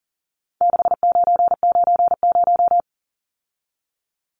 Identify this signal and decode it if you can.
Morse code, heard as a keyed tone, about 40 wpm, 700 Hz. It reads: 6990
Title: Numbers - 4-Digit 40wpm